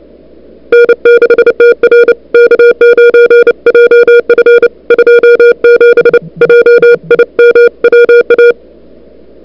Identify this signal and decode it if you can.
Morse code, heard as a keyed tone, about 29 wpm, 1430 Hz. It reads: N6TR K9JF 27 JIM WA
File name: CWSprint / February 2007 CW Sprint / 01-2007 CW Sprint Hour1